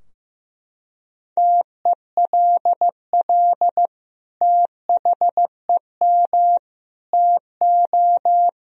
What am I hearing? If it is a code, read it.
Morse code, heard as a keyed tone, about 15 wpm, 700 Hz. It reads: TELL THEM TO
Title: Sentences from Top 1000 Words - Review Part 3 - 15wpm